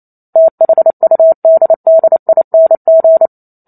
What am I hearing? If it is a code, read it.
Morse code, heard as a keyed tone, about 28 wpm, 650 Hz. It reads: THUDDING